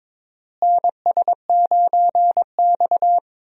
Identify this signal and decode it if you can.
Morse code, heard as a keyed tone, about 22 wpm, 700 Hz. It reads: NS9X
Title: USA Call Signs - 4 Characters - Repeated 2x - 22wpm